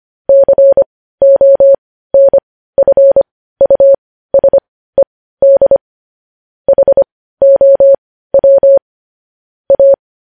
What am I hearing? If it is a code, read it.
Morse code, heard as a keyed tone, about 25 wpm, spaced as for 15 wpm, 550 Hz. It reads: CONFUSED HOW A
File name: podcast / News headlines-1x-25x15